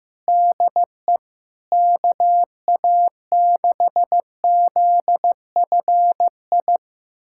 Morse code, heard as a keyed tone, about 15 wpm, 700 Hz. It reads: DE KA6ZFI
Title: USA Callsign DE Callsign - 6 Characters - 15wpm